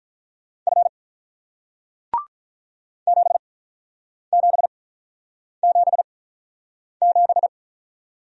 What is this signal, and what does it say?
Morse code, heard as a keyed tone, about 48 wpm, spaced as for 9 wpm, 700 Hz. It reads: F 7777